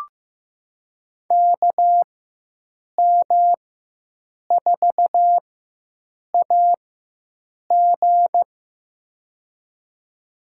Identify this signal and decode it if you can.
Morse code, heard as a keyed tone, about 15 wpm, spaced as for 7 wpm, 700 Hz. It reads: KM4AG
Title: USA Call Signs - 4x Character Spacing - 15wpm